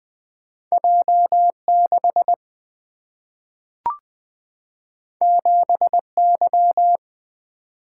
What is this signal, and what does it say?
Morse code, heard as a keyed tone, about 20 wpm, 700 Hz. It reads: J6 7Y
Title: Single-Letter Single-Number 20wpm